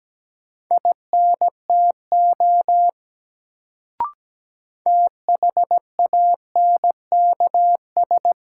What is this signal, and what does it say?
Morse code, heard as a keyed tone, about 17 wpm, 700 Hz. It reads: INTO THANKS
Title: Sentences from Top 600 Words - New Word Drill - 17wpm